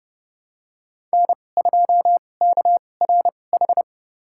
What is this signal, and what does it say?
Morse code, heard as a keyed tone, about 30 wpm, spaced as for 22 wpm, 690 Hz. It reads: N2KRH
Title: USA Call Signs - 5 Characters - 2x Character Spacing - 30wpm